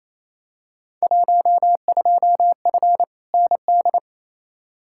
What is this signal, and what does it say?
Morse code, heard as a keyed tone, about 28 wpm, 700 Hz. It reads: 12F ND
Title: Field Day Exchanges Encore - 28wpm